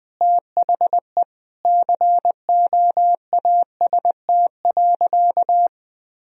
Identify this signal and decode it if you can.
Morse code, heard as a keyed tone, about 21 wpm, 700 Hz. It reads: THE COAST.
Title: Sentences from Top 1000 Words - Review Part 4 - 20wpm